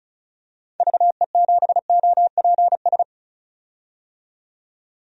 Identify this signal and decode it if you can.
Morse code, heard as a keyed tone, about 36 wpm, 700 Hz. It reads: VE7OPS
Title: DXCC-100 Call Signs - 6 Characters - 35wpm